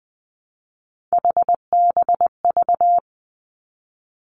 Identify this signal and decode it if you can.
Morse code, heard as a keyed tone, about 20 wpm, 700 Hz. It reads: HBV
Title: Three Challenging Letter-Number Combinations 20wpm